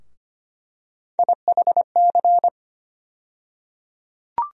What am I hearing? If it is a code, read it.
Morse code, heard as a keyed tone, about 25 wpm, 700 Hz. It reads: IHC
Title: USA Call Sign Postfixes - 3 Letters 25wpm